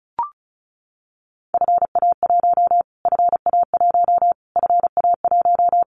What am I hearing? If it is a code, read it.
Morse code, heard as a keyed tone, about 34 wpm, 700 Hz. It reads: FA1 FA1 FA1
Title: Callsign Prefixes for Top 25 DXCC - Repeated 3x - 35wpm